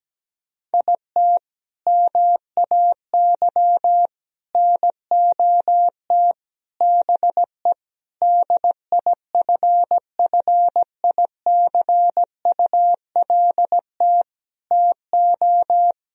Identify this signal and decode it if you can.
Morse code, heard as a keyed tone, about 17 wpm, 700 Hz. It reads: IT MAY NOT BE DIFFICULT TO
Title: Sentences from Top 1000 Words - Review Part 4 - 17wpm